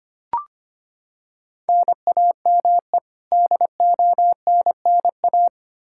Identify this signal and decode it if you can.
Morse code, heard as a keyed tone, about 25 wpm, 700 Hz. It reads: NAME DONNA
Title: Practice QSO - Exchange 1 - Beginning - 25wpm